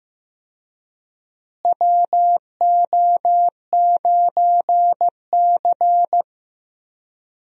Morse code, heard as a keyed tone, about 15 wpm, 700 Hz. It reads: WO9C
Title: USA Call Signs - 4 Characters - Repeated 3x - 15wpm